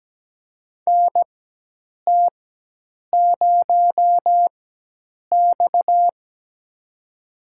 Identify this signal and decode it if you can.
Morse code, heard as a keyed tone, about 17 wpm, spaced as for 8 wpm, 700 Hz. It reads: NT0X
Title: USA Call Signs - 4 Characters - 4x Character Spacing - 17wpm